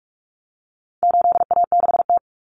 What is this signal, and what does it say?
Morse code, heard as a keyed tone, about 44 wpm, 700 Hz. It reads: 8A6T
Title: Common Cut Numbers - 4-Digit 45wpm